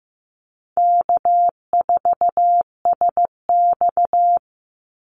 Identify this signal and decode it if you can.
Morse code, heard as a keyed tone, about 15 wpm, 700 Hz. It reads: K4SX
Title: Easy Call Signs - 15wpm